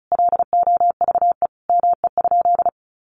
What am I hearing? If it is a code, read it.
Morse code, heard as a keyed tone, about 34 wpm, 700 Hz. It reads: LOVE ME?
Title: Sentences from Top 400 Words 35wpm